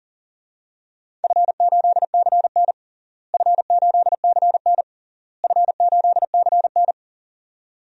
Morse code, heard as a keyed tone, about 40 wpm, 700 Hz. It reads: F8CN F8CN F8CN
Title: DXCC-100 Call Signs - 4 Characters - Repeated 3x - 40wpm